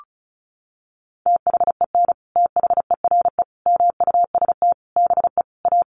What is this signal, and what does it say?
Morse code, heard as a keyed tone, about 34 wpm, 700 Hz. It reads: THEN THERE MUST BE A
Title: Sentences from Top 1400 Words 35wpm